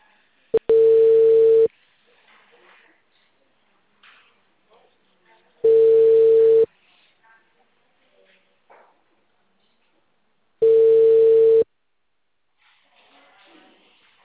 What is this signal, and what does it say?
Ringback tone (China, 450 Hz).
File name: out- -2039- - - .50035